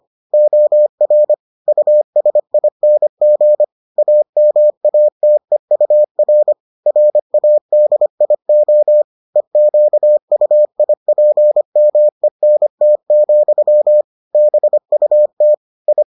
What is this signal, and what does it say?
Morse code, heard as a keyed tone, about 25 wpm, 600 Hz. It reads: OR USING AMATEUR RADIO EQUIPMENT, BUT I